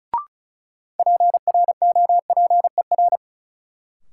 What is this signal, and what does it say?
Morse code, heard as a keyed tone, about 36 wpm, 700 Hz. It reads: PROPER
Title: Sentences from Top 800 Words - New Word Drill - 35wpm